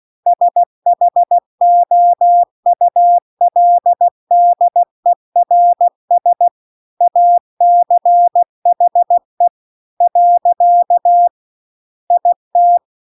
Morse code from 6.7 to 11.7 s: ACHE .